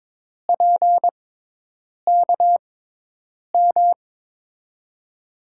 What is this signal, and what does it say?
Morse code, heard as a keyed tone, about 21 wpm, spaced as for 8 wpm, 700 Hz. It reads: PKM
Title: USA Call Signs - 6 Characters - 6x Character Spacing - 22wpm